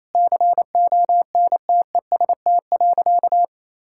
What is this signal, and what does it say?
Morse code, heard as a keyed tone, about 28 wpm, 700 Hz. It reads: CONTEST.
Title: Sentences from Top 1600 Words 28wpm